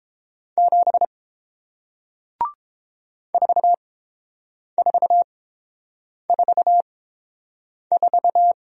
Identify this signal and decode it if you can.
Morse code, heard as a keyed tone, about 30 wpm, spaced as for 7 wpm, 700 Hz. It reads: 7 4444